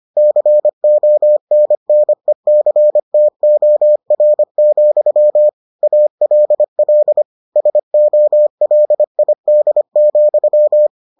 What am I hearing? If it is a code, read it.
Morse code, heard as a keyed tone, about 25 wpm, 600 Hz. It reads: CONNECTOR, ALL SOLID,